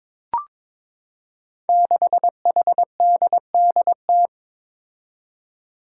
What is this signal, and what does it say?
Morse code, heard as a keyed tone, about 22 wpm, 700 Hz. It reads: 6HDDT
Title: Five Challenging Letter-Number Combinations 22wpm